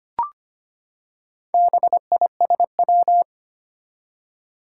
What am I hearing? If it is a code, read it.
Morse code, heard as a keyed tone, about 25 wpm, 700 Hz. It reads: BISW
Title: Four Challenging Letter-Number Combinations 25wpm